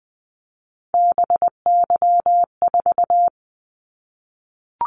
Morse code, heard as a keyed tone, about 20 wpm, 700 Hz. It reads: BY4
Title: Callsign Prefixes for Top 100 DXCC - 20wpm